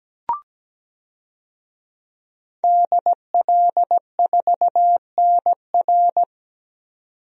Morse code, heard as a keyed tone, about 17 wpm, 700 Hz. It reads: DL4NR
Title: DXCC-100 Call Signs - 5 Characters - 17wpm